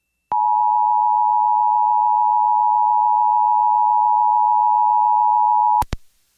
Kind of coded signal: Emergency Alert System (SAME) tones